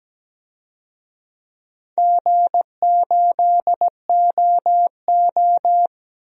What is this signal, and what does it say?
Morse code, heard as a keyed tone, about 17 wpm, 700 Hz. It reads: G8OO
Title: DXCC-100 Call Signs - 4 Characters - 17wpm